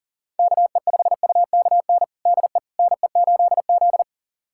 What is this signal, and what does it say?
Morse code, heard as a keyed tone, about 40 wpm, 700 Hz. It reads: KE5UKN DE NE8Z
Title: USA Callsign DE Callsign 40wpm